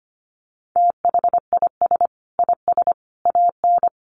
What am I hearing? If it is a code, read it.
Morse code, heard as a keyed tone, about 25 wpm, 700 Hz. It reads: THIS IS AN
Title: Sentences from Top 900 Words 25wpm